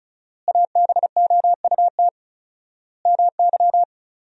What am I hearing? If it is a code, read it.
Morse code, heard as a keyed tone, about 36 wpm, 700 Hz. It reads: ABOUT MY
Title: 4x Word-Spacing - Sentences from Top 400 Words - 35wpm